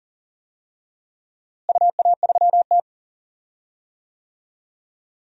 Morse code, heard as a keyed tone, about 40 wpm, 700 Hz. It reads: UA3T
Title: DXCC-100 Call Signs - 4 Characters - Repeated 3x - 40wpm